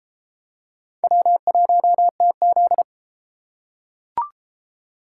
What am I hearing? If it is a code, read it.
Morse code, heard as a keyed tone, about 33 wpm, 700 Hz. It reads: W1TZ